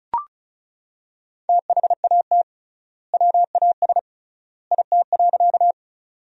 Morse code, heard as a keyed tone, about 34 wpm, 700 Hz. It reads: THAT WAS IT.
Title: 3x Word-Spacing - Sentences from Top 100 Words - 35wpm